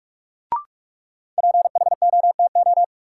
Morse code, heard as a keyed tone, about 47 wpm, 700 Hz. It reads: PHOTO
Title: Sentences from Top 1200 Words - New Word Drill Alt - 45wpm